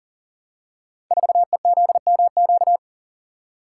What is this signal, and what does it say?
Morse code, heard as a keyed tone, about 41 wpm, 700 Hz. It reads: 4EZMQ
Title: Code Groups - 5 Letter:Numbers - 40wpm